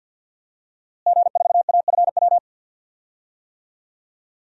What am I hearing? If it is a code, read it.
Morse code, heard as a keyed tone, about 48 wpm, 700 Hz. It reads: G4AFW
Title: DXCC-100 Call Signs - 5 Characters - Repeated 3x - 50wpm